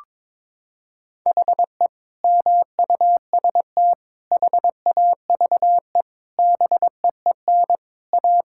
Morse code, heard as a keyed tone, about 22 wpm, 700 Hz. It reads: HE MUST HAVE BEEN A
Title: Sentences from Top 1600 Words 22wpm